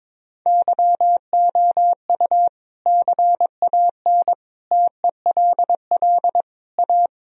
Morse code, heard as a keyed tone, about 22 wpm, 700 Hz. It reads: YOU CAN TELL A
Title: Sentences from Top 1000 Words - Review Part 3 - 22wpm